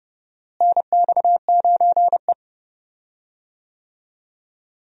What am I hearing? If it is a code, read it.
Morse code, heard as a keyed tone, about 30 wpm, 700 Hz. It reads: NX9E
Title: USA Call Signs - 4 Characters - Repeated 3x - 30wpm